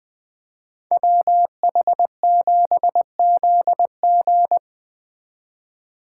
Morse code, heard as a keyed tone, about 21 wpm, 700 Hz. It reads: WH7ZG